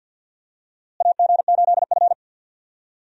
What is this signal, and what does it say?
Morse code, heard as a keyed tone, about 49 wpm, 700 Hz. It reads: AG8F
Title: USA Call Signs - 4 Characters - Repeated 3x - 50wpm